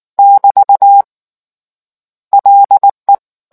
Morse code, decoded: <BT> LE

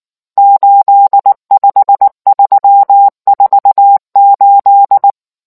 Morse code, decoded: 85348

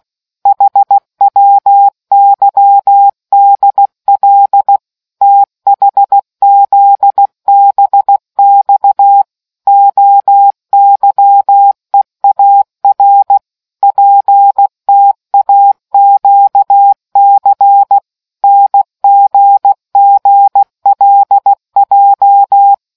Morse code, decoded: HWYDL THZBX OYEAR PTAQC NGGLJ

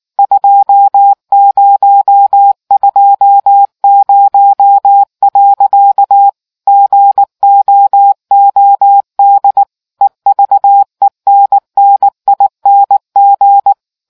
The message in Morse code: 2020. GOOD EVENNING

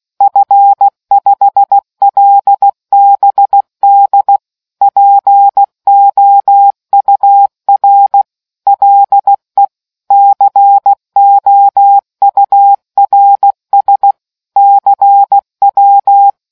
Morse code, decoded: F5LBD POUR LE COURS CW

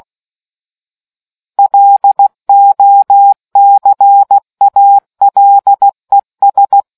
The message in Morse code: LOCALES

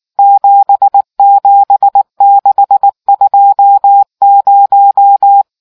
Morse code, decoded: 77620

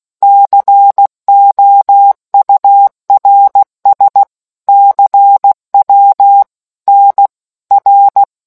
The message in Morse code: COURS CW NR

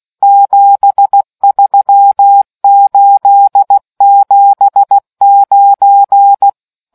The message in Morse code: 73879

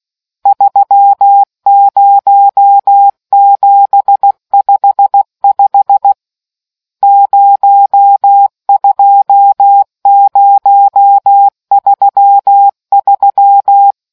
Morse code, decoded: 30755 02033

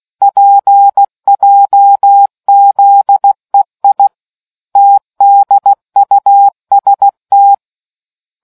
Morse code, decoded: PJZEI TDUST

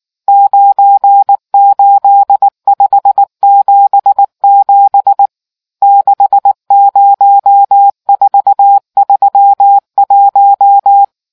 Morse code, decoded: 98577 60431

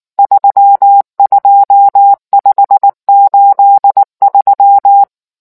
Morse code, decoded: 32583

